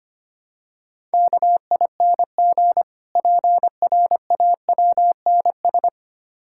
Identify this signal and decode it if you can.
Morse code, heard as a keyed tone, about 25 wpm, 690 Hz. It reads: KING PRAWNS
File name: practice / 25wpm / Difficult Words with Hints - 25wpm